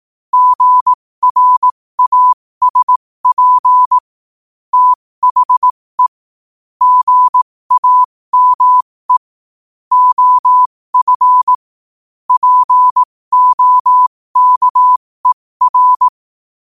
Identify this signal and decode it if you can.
Morse code, heard as a keyed tone, about 18 wpm, spaced as for 15 wpm, 1000 Hz. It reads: GRASP THE GAME OF POKER
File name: qotdfiles / 15wpm